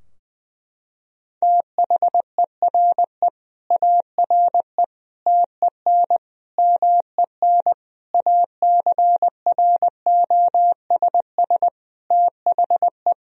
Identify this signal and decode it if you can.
Morse code, heard as a keyed tone, about 20 wpm, 700 Hz. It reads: THERE ARE TEN MEN ACROSS THE
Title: Sentences from Top 1000 Words - Review Part 2 - 20wpm